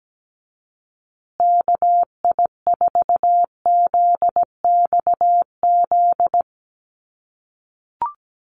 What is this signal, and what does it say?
Morse code, heard as a keyed tone, about 17 wpm, 700 Hz. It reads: KI4ZXZ
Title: USA Call Signs Repeated 2x 17wpm